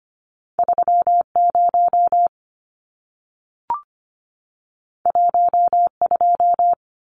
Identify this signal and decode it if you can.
Morse code, heard as a keyed tone, about 25 wpm, 700 Hz. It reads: 30 12
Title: Numbers - 2-Digit 25wpm